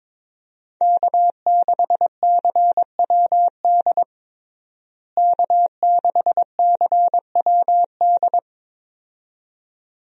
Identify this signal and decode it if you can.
Morse code, heard as a keyed tone, about 22 wpm, 700 Hz. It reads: K6CWD K6CWD
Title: USA Call Signs - 5 Characters - Repeated 2x - 22wpm